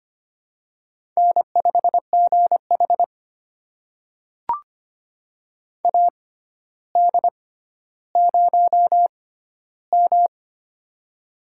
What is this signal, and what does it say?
Morse code, heard as a keyed tone, about 25 wpm, spaced as for 13 wpm, 700 Hz. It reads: N5GH AD0M
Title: USA Call Signs - 4 Characters - 6x Character Spacing - 25wpm